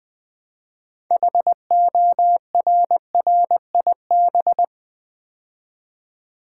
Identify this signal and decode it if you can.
Morse code, heard as a keyed tone, about 20 wpm, 700 Hz. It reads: HORRIB
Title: Build A Word - 2 Letters At A Time - 8-Letter Words - 20wpm